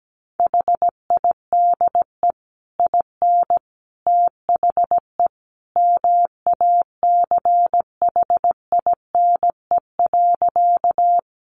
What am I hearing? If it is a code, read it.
Morse code, heard as a keyed tone, about 17 wpm, 700 Hz. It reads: HIDE IN THE MACHINE.
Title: A Sound Of Thunder 17wpm